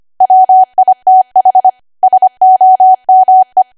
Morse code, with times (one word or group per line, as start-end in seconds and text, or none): 0.2-1.7 WITH
2.0-3.6 SOME